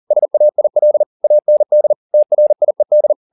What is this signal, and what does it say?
Morse code, heard as a keyed tone, about 40 wpm, 600 Hz. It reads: SAIL AND TRIED